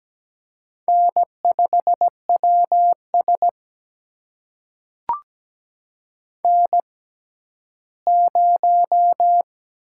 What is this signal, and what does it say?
Morse code, heard as a keyed tone, about 17 wpm, spaced as for 12 wpm, 700 Hz. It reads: N5WS N0
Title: USA Call Signs - 6x Character Spacing - 17wpm